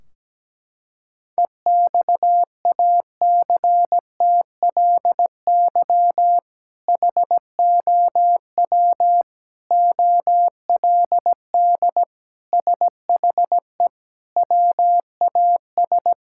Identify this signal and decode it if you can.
Morse code, heard as a keyed tone, about 17 wpm, 700 Hz. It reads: EXACTLY HOW OLD SHE WAS
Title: Sentences from Top 1600 Words 17wpm